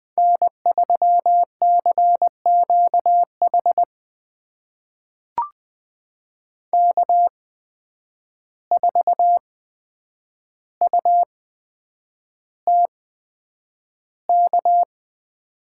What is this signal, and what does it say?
Morse code, heard as a keyed tone, about 20 wpm, spaced as for 9 wpm, 700 Hz. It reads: N3CQH K4UTK
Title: USA Call Signs - 5 Characters - 8x Character Spacing - 20wpm